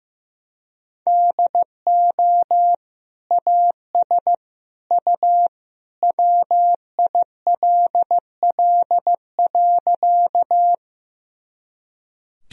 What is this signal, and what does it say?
Morse code, heard as a keyed tone, about 15 wpm, 700 Hz. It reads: DO AS U WILL.
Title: Sentences from Top 100 Words and Abbreviations 15wpm